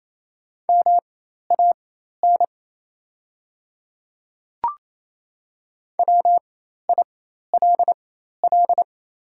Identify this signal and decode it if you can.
Morse code, heard as a keyed tone, about 28 wpm, spaced as for 13 wpm, 690 Hz. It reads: MAN WILL
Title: 4x Character Spacing - Top 300 Words - 28wpm